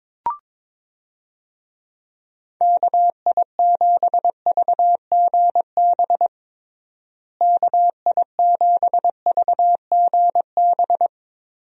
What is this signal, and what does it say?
Morse code, heard as a keyed tone, about 22 wpm, 700 Hz. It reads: KI7VGB KI7VGB
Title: USA Call Signs - 6 Characters - Repeated 2x - 22wpm